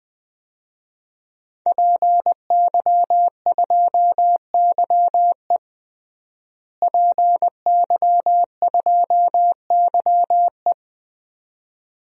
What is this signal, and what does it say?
Morse code, heard as a keyed tone, about 20 wpm, 700 Hz. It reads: PY2YE PY2YE
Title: DXCC-100 Call Signs - 5 Characters - Repeated 2x - 20wpm